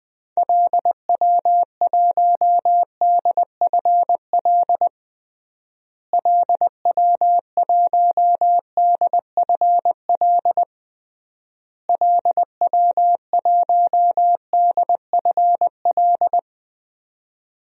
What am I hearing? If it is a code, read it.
Morse code, heard as a keyed tone, about 20 wpm, 700 Hz. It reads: LW1DFL LW1DFL LW1DFL
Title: DXCC-100 Call Signs - 6 Characters - Repeated 3x - 20wpm